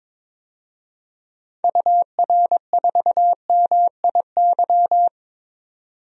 Morse code, heard as a keyed tone, about 21 wpm, 700 Hz. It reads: UR4MIY
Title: DXCC-100 Call Signs - 6 Characters - 22wpm